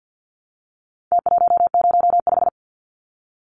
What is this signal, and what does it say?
Morse code, heard as a keyed tone, about 50 wpm, 700 Hz. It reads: T105
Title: Numbers - Contest Style - 4-Digit 50wpm